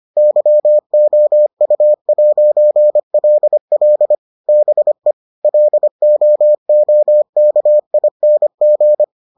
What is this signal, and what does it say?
Morse code, heard as a keyed tone, about 25 wpm, 600 Hz. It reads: YOU'LL BE LOOKING